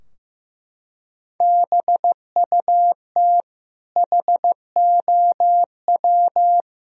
Morse code, heard as a keyed tone, about 15 wpm, 700 Hz. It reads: BUT HOW
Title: Sentences from Top 1000 Words - Review Part 3 - 15wpm